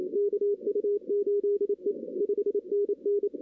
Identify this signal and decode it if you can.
Morse code, heard as a keyed tone, about 26 wpm, 410 Hz. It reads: KU8E 5NN
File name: 2016 arrl dx cw / 2016-ARRL-DX-CW-K5TR- - 0-9